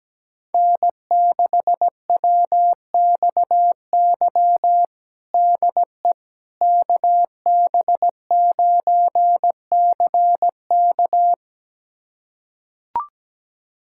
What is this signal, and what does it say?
Morse code, heard as a keyed tone, about 17 wpm, 700 Hz. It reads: N6WXY DE KB9CK
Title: USA Callsign DE Callsign - 5 Characters - 17wpm